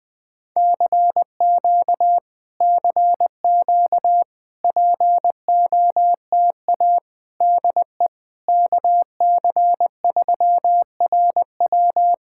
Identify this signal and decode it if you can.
Morse code, heard as a keyed tone, about 21 wpm, 700 Hz. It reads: CQ CQ POTA DE KC3RW
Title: POTA QSO Practice - 20wpm